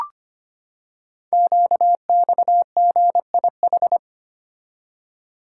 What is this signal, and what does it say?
Morse code, heard as a keyed tone, about 25 wpm, 700 Hz. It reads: QXGIH
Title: Code Groups - 5 Letters - 25wpm